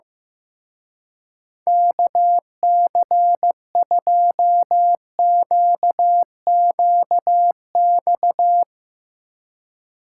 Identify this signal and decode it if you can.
Morse code, heard as a keyed tone, about 15 wpm, 700 Hz. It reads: KC2QQX
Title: Speed Racing - USA 2x3 Call Signs 15wpm